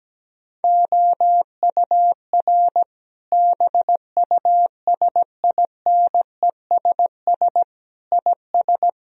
Morse code, heard as a keyed tone, about 17 wpm, 700 Hz. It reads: OUR BUSINESS IS
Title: Sentences from Top 1000 Words - Review Part 4 - 17wpm